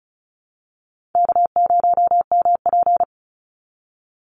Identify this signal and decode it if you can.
Morse code, heard as a keyed tone, about 34 wpm, 700 Hz. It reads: K0MP
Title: Easy Call Signs Repeated 3x - 35wpm